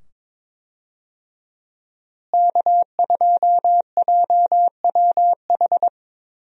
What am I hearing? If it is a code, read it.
Morse code, heard as a keyed tone, about 22 wpm, 700 Hz. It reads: K2JWH
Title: USA Call Signs - 5 Characters - Repeated 2x - 22wpm